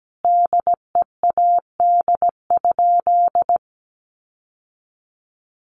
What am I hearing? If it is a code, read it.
Morse code, heard as a keyed tone, about 17 wpm, 700 Hz. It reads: DEAD?
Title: Sentences from Top 1500 Words 17wpm